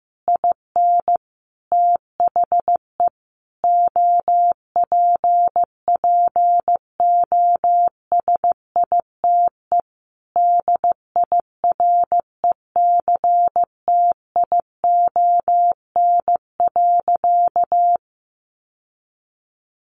Morse code, heard as a keyed tone, about 15 wpm, 700 Hz. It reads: IN THE OPPOSITE DIRECTION.
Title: Sentences from Top 1300 Words 15wpm